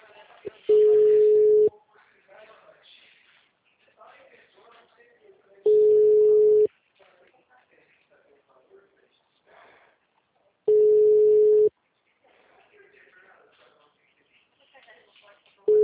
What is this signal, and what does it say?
Ringback tone (Europe, 425 Hz), with speech.
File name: out- -2001- - - .1692